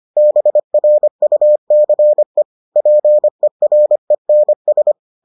Morse code, heard as a keyed tone, about 25 wpm, 600 Hz. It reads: BRUCE PERENS